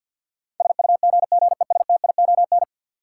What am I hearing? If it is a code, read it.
Morse code, heard as a keyed tone, about 48 wpm, 700 Hz. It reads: SUGGESTION
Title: Sentences from Top 1600 Words - New Word Drill - 50wpm